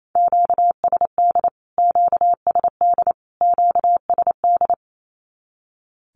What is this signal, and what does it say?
Morse code, heard as a keyed tone, about 28 wpm, 700 Hz. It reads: QSD QSD QSD
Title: Q-Codes Repeated 3x 28wpm